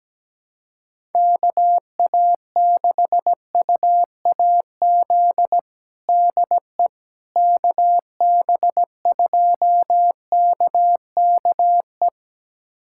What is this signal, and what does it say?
Morse code, heard as a keyed tone, about 17 wpm, 700 Hz. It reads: KA6UAZ DE KB2KKE
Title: USA Callsign DE Callsign - 6 Characters - 17wpm